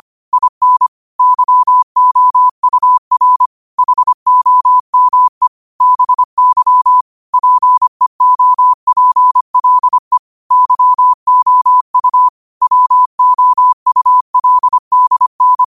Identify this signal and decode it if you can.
Morse code, heard as a keyed tone, about 25 wpm, 1000 Hz. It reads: IN YOUR HOME BY PEOPLE YOU WOULDN
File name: qotdfiles / 25wpm